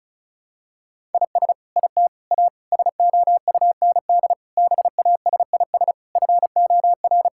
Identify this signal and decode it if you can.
Morse code, heard as a keyed tone, about 36 wpm, 700 Hz. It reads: IS IT A SOUND BASIS FOR
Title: Sentences from Top 1000 Words - Review Part 5 - 35wpm